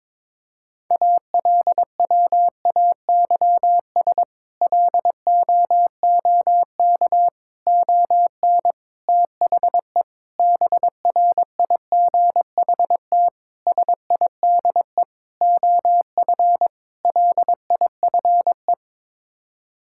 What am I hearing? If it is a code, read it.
Morse code, heard as a keyed tone, about 21 wpm, 700 Hz. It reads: ALWAYS LOOK ON THE BRIGHT SIDE OF LIFE
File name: practice / 22wpm / Sets of 8 Words - 22wpm